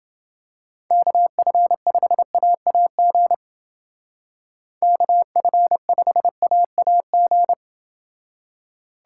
Morse code, heard as a keyed tone, about 29 wpm, 700 Hz. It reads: KF5AAG KF5AAG